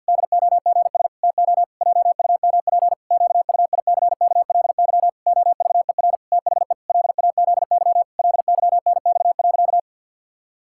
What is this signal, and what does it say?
Morse code, heard as a keyed tone, about 50 wpm, 700 Hz. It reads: DOGS TO JUMP QUICKLY OVER THE LAZY LYNX.
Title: Pangram Challenge v2 - 50wpm